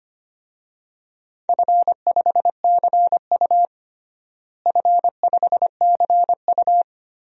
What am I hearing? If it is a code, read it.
Morse code, heard as a keyed tone, about 25 wpm, 700 Hz. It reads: F5CU F5CU
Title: DXCC-100 Call Signs - 4 Characters - Repeated 3x - 25wpm